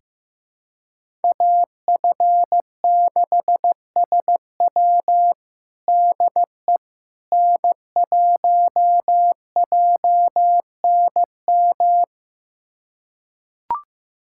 Morse code, heard as a keyed tone, about 15 wpm, 700 Hz. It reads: AF6SW DE N1JNM
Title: USA Callsign DE Callsign - 5 Characters - 15wpm